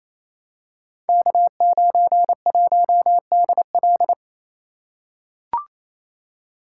Morse code, heard as a keyed tone, about 27 wpm, 700 Hz. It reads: K91DL
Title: Code Groups - 5 Letter:Numbers - 28wpm